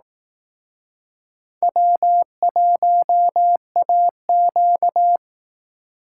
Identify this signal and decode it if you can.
Morse code, heard as a keyed tone, about 18 wpm, 700 Hz. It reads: W1AQ
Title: Speed Racing - USA 1x2 Call Signs 15wpm